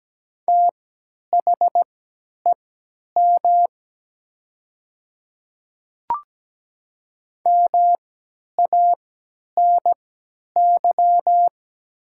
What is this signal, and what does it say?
Morse code, heard as a keyed tone, about 17 wpm, spaced as for 10 wpm, 700 Hz. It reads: THEM MANY
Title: 3x Character Spacing - Top 100 Words - 17wpm